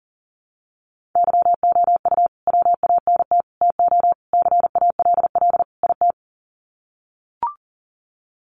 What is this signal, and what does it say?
Morse code, heard as a keyed tone, about 40 wpm, 700 Hz. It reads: YOU WANT TO CALL IT
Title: Sets of 5 Words Encore2 40wpm